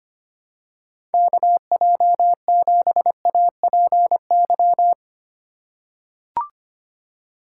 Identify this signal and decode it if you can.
Morse code, heard as a keyed tone, about 25 wpm, 700 Hz. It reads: KJ7APY
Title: USA Call Signs - 6 Characters - 4x Character Spacing - 25wpm